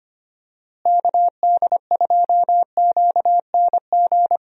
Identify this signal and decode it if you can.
Morse code, heard as a keyed tone, about 25 wpm, 700 Hz. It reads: KD2QNG
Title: USA Call Signs - 6 Characters - Repeated 3x - 25wpm